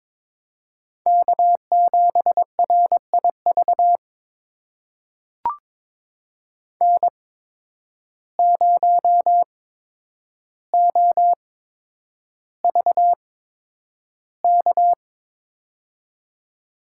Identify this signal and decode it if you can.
Morse code, heard as a keyed tone, about 22 wpm, spaced as for 9 wpm, 700 Hz. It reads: K7RIV N0OVK
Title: USA Call Signs - 5 Characters - 8x Character Spacing - 22wpm